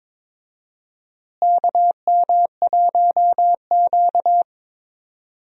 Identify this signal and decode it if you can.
Morse code, heard as a keyed tone, about 22 wpm, 700 Hz. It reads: KM1Q
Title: USA Call Signs - 4 Characters - Repeated 3x - 22wpm